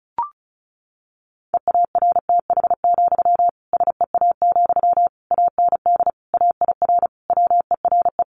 Morse code, heard as a keyed tone, about 34 wpm, 700 Hz. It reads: EARTH, SEA, AND AIR WERE
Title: Sentences from Top 1500 Words 35wpm